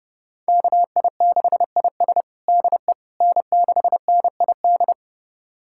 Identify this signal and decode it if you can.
Morse code, heard as a keyed tone, about 32 wpm, 700 Hz. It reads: KI6IS DE N6NID
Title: USA Callsign DE Callsign - 5 Characters - 30wpm